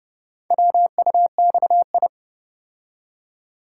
Morse code, heard as a keyed tone, about 29 wpm, 700 Hz. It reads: WUXI